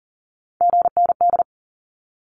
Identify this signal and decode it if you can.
Morse code, heard as a keyed tone, about 40 wpm, 700 Hz. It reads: GND
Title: Abbreviations 40wpm